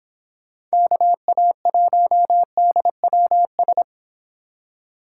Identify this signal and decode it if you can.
Morse code, heard as a keyed tone, about 26 wpm, 700 Hz. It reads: KA1DWS